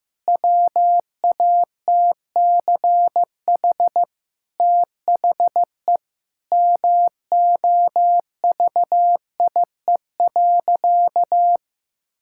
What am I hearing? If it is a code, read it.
Morse code, heard as a keyed tone, about 15 wpm, 700 Hz. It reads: WATCH THE MOVIE.
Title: Sentences from Top 1000 Words - Review Part 5 - 15wpm